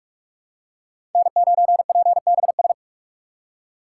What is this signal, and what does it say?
Morse code, heard as a keyed tone, about 46 wpm, 700 Hz. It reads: N9PBS
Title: USA Call Signs - 5 Characters - 2x Character Spacing - 45wpm